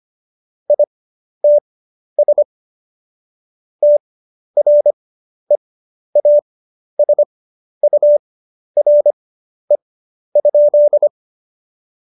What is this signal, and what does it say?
Morse code, heard as a keyed tone, about 25 wpm, spaced as for 11 wpm, 590 Hz. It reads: ITS TREASURE?
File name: Jokes-25-char-10-eff